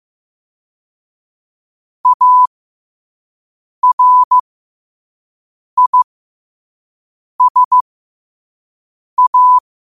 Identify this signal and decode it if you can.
Morse code, heard as a keyed tone, about 15 wpm, spaced as for 5 wpm, 1000 Hz. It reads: ARISA
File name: qotdfiles / 5wpm